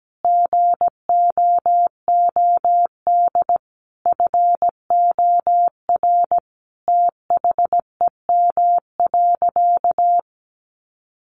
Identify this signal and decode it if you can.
Morse code, heard as a keyed tone, about 17 wpm, 700 Hz. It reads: GOOD FOR THEM.
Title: Sentences from Top 1300 Words 17wpm